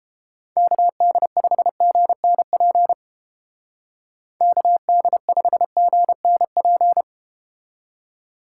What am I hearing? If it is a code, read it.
Morse code, heard as a keyed tone, about 31 wpm, 700 Hz. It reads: KD5GNP KD5GNP